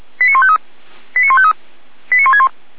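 Five-tone selective call (ZVEI), heard as quick runs of five tones, tones of about 70 ms. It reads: 89153 89153 89161
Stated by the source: ZVEI five-tone call received on an SDR.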